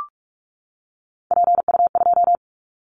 Morse code, heard as a keyed tone, about 44 wpm, 700 Hz. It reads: PU2
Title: Callsign Prefixes for Top 100 DXCC - 45wpm